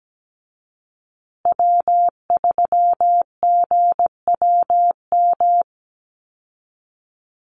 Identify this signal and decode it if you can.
Morse code, heard as a keyed tone, about 17 wpm, 700 Hz. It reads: W3GWM
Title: Call Signs Repeated 2x - 17wpm